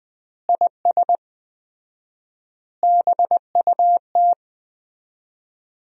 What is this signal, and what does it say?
Morse code, heard as a keyed tone, about 20 wpm, 700 Hz. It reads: IS BUT
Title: 4x Word-Spacing - Sentences from Top 100 Words - 20wpm